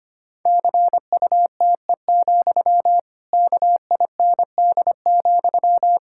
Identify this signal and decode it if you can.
Morse code, heard as a keyed tone, about 25 wpm, 700 Hz. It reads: CUTE, KIND,
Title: Pangram Challenge v2 - 25wpm